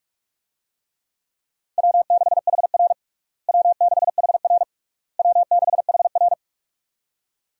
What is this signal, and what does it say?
Morse code, heard as a keyed tone, about 47 wpm, 700 Hz. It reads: W6HR W6HR W6HR
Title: USA Call Signs - 4 Characters - Repeated 3x - 45wpm